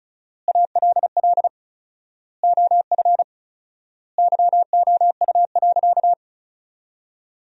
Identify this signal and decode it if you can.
Morse code, heard as a keyed tone, about 34 wpm, 700 Hz. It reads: ALL OF YOU.
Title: 4x Word-Spacing - Sentences from Top 500 Words - 35wpm